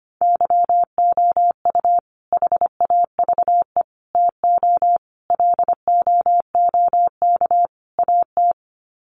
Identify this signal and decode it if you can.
Morse code, heard as a keyed tone, about 25 wpm, 700 Hz. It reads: YOU HAVE TO LOOK AT
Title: Sets of 5 Words Encore 25wpm